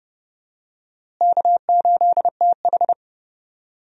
Morse code, heard as a keyed tone, about 30 wpm, 700 Hz. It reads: K8TH
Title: USA Call Signs - 4 Characters - Repeated 3x - 30wpm